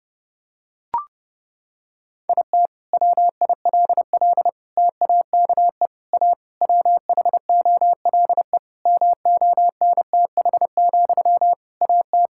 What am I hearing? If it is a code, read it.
Morse code, heard as a keyed tone, about 29 wpm, 700 Hz. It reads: IT WILL TAKE A WHOLE MONTH, AT
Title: Sentences from Top 1000 Words - Review Part 5 - 30wpm